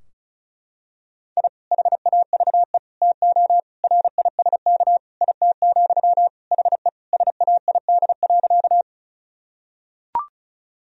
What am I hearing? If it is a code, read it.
Morse code, heard as a keyed tone, about 36 wpm, 700 Hz. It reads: I HAVE TO RISK IT, HE SAID.
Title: Sentences from Top 1000 Words - Review Part 4 - 35wpm